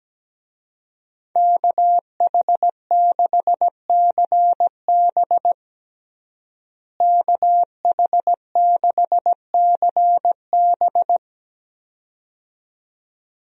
Morse code, heard as a keyed tone, about 17 wpm, 700 Hz. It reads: KH6CB KH6CB
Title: USA Call Signs - 5 Characters - Repeated 2x - 17wpm